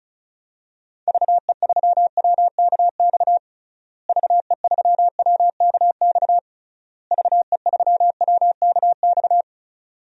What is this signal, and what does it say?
Morse code, heard as a keyed tone, about 34 wpm, 700 Hz. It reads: VE3WKX VE3WKX VE3WKX
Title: DXCC-100 Call Signs - 6 Characters - Repeated 3x - 35wpm